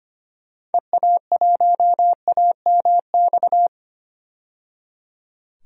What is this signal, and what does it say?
Morse code, heard as a keyed tone, about 25 wpm, 700 Hz. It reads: EA1AMX
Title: DXCC-100 Call Signs - 6 Characters - 25wpm